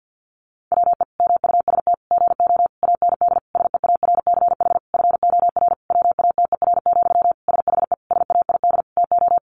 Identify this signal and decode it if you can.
Morse code, heard as a keyed tone, about 50 wpm, 700 Hz. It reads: WE MUST GO AND SEARCH FOR WATER, SHE SAID TO
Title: Sentences from Top 800 Words 50wpm